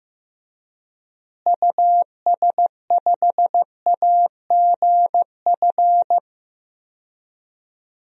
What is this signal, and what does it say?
Morse code, heard as a keyed tone, about 15 wpm, 700 Hz. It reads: US5AGF
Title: DXCC-100 Call Signs - 6 Characters - Repeated 2x - 15wpm